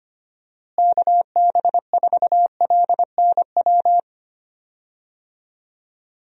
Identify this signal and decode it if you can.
Morse code, heard as a keyed tone, about 25 wpm, 700 Hz. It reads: KB4LNW
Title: USA Call Signs - 6 Characters - Repeated 2x - 25wpm